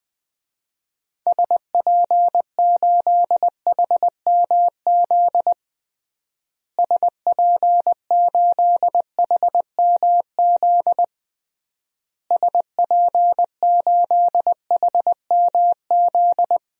Morse code, heard as a keyed tone, about 20 wpm, 700 Hz. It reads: SP8HMZ SP8HMZ SP8HMZ
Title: DXCC-100 Call Signs - 6 Characters - Repeated 3x - 20wpm